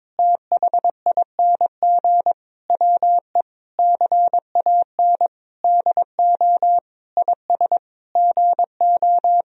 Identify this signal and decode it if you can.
Morse code, heard as a keyed tone, about 21 wpm, 700 Hz. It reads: THING WE CAN DO IS GO
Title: Sentences from Top 200 Words and Abbreviations 22wpm